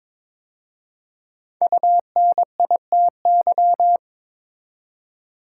Morse code, heard as a keyed tone, about 22 wpm, 700 Hz. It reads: UNITY
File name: 5 Letter Words - First 3 Letters Given - 22wpm